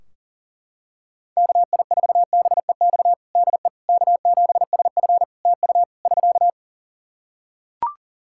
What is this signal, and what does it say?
Morse code, heard as a keyed tone, about 40 wpm, 700 Hz. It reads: KI4BEX DE K7SF TU <SK>
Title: Practice QSO - Exchange 4 - Ending - 40wpm